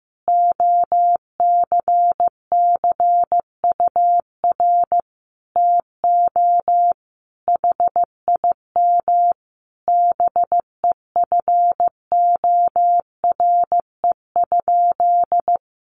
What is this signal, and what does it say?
Morse code, heard as a keyed tone, about 15 wpm, 700 Hz. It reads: OCCUR TO HIM BEFORE?
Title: Sentences from Top 900 Words 15wpm